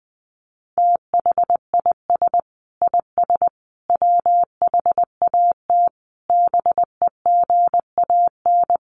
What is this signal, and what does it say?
Morse code, heard as a keyed tone, about 20 wpm, 700 Hz. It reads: THIS IS WHAT BEGAN
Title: Sentences from Top 300 Words 20wpm